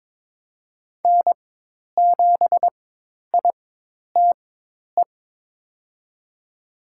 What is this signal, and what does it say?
Morse code, heard as a keyed tone, about 21 wpm, spaced as for 10 wpm, 700 Hz. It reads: N7ITE
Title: USA Call Signs - 5 Characters - 4x Character Spacing - 22wpm